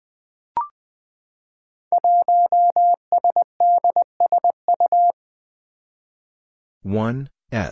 Morse code, heard as a keyed tone, about 20 wpm, 700 Hz. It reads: 1SDSU
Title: Five Challenging Letter-Number Combinations 20wpm